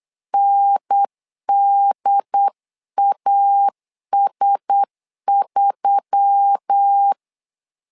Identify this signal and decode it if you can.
Morse code, heard as a keyed tone, about 8 wpm, 800 Hz. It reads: NDAS3
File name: LEZIONI / 5 LBVC / 5 LB 8wpm 2816